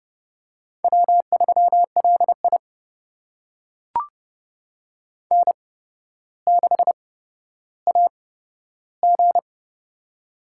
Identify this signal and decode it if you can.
Morse code, heard as a keyed tone, about 30 wpm, spaced as for 13 wpm, 700 Hz. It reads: W3LI N6AG
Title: USA Call Signs - 8x Character Spacing - 30wpm